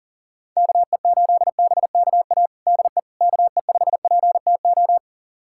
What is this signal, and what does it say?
Morse code, heard as a keyed tone, about 37 wpm, 690 Hz. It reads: KE8BKA DE KE5PTO
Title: USA Callsign DE Callsign - 6 Characters - 40wpm